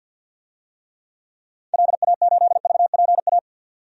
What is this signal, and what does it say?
Morse code, heard as a keyed tone, about 50 wpm, 700 Hz. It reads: LA8VPA
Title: DXCC-100 Call Signs - 6 Characters - 50wpm